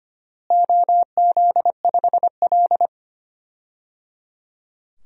Morse code, heard as a keyed tone, about 25 wpm, 700 Hz. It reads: OZ5L
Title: DXCC-100 Call Signs - 4 Characters - Repeated 3x - 25wpm